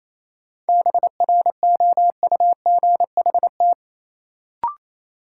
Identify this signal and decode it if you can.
Morse code, heard as a keyed tone, about 28 wpm, 700 Hz. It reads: BROUGHT
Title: Sentences from Top 500 Words - New Word Drill - 28wpm